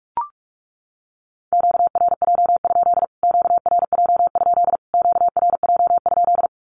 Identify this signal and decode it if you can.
Morse code, heard as a keyed tone, about 44 wpm, 700 Hz. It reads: QRJ? QRJ? QRJ?
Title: Q-Codes Repeated 3x 45wpm